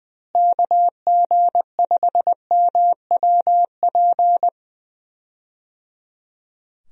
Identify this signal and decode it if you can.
Morse code, heard as a keyed tone, about 21 wpm, 700 Hz. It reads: KG5MWP
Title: USA Call Signs - 6 Characters - Repeated 3x - 20wpm